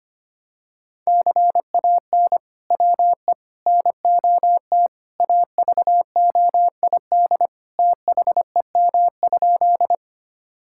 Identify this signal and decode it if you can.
Morse code, heard as a keyed tone, about 25 wpm, 700 Hz. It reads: CAN WE NOT AVOID THEM?
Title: Sentences from Top 1000 Words - Review Part 6 - 25wpm